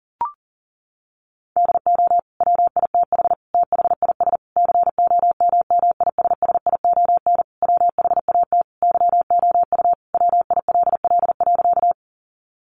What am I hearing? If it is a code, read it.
Morse code, heard as a keyed tone, about 37 wpm, 700 Hz. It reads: DO WITH THIS COMMISSION WHAT YOU WILL.
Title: Sentences from Top 1400 Words 40wpm